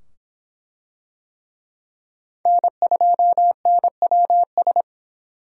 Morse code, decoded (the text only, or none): N2NWS